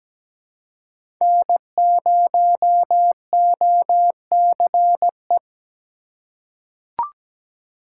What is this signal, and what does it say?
Morse code, heard as a keyed tone, about 17 wpm, 700 Hz. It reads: N0OCE
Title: USA Call Signs - 4x Character Spacing - 17wpm